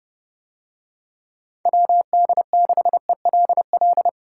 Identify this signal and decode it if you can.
Morse code, heard as a keyed tone, about 27 wpm, 700 Hz. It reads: WD6ELL